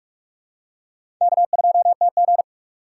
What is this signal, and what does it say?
Morse code, heard as a keyed tone, about 45 wpm, 700 Hz. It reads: K2TG
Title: USA Call Signs - 4 Characters - 2x Character Spacing - 45wpm